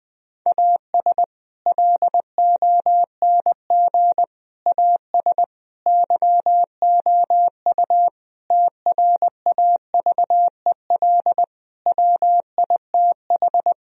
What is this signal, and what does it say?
Morse code, heard as a keyed tone, about 20 wpm, 700 Hz. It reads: AS LONG AS YOU TRAVEL WITH
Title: Sentences from Top 1000 Words - Review Part 2 - 20wpm